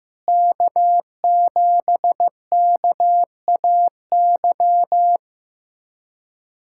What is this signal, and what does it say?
Morse code, heard as a keyed tone, about 15 wpm, 700 Hz. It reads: K7KAY
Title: USA Callsign DE Callsign - 5 Characters - 15wpm